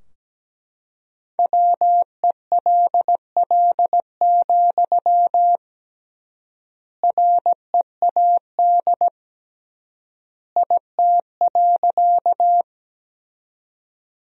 Morse code, heard as a keyed tone, about 17 wpm, 700 Hz. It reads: WELL, READ IT.
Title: 3x Word-Spacing - Sentences from Top 200 Words - 17wpm